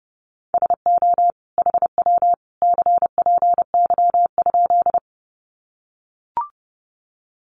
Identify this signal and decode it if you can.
Morse code, heard as a keyed tone, about 30 wpm, 700 Hz. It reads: SO HW CPY?
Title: Common QSO Phrases 30wpm